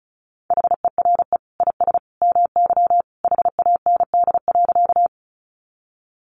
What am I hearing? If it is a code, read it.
Morse code, heard as a keyed tone, about 34 wpm, 700 Hz. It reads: HERE IS MY HAND.
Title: Sentences from Top 200 Words 35wpm